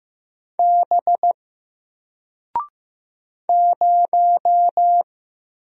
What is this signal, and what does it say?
Morse code, heard as a keyed tone, about 15 wpm, 700 Hz. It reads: B 0
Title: Single Letter-Number 15wpm